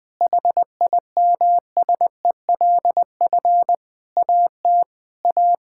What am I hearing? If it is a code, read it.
Morse code, heard as a keyed tone, about 20 wpm, 700 Hz. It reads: HIMSELF AT A
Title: Sentences from Top 1600 Words 20wpm